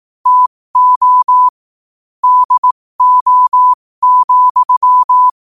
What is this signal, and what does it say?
Morse code, heard as a keyed tone, about 18 wpm, spaced as for 15 wpm, 1000 Hz. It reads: TO DO,
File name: qotdfiles / 15wpm